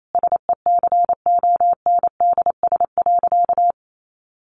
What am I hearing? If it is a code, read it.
Morse code, heard as a keyed tone, about 28 wpm, 700 Hz. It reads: SECONDS.
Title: A Sound Of Thunder 28wpm